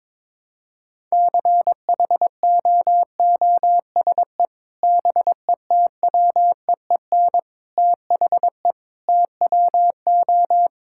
Morse code, heard as a keyed tone, about 22 wpm, 700 Hz. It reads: CHOOSE BETWEEN THE TWO
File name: practice / 22wpm / Sentences with 3-Word Phrases - 22wpm